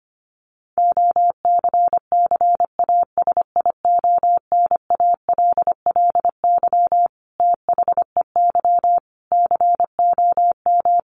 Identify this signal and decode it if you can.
Morse code, heard as a keyed tone, about 25 wpm, 700 Hz. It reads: OCCASIONALLY THEY COM
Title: Sentences from Top 1300 Words 25wpm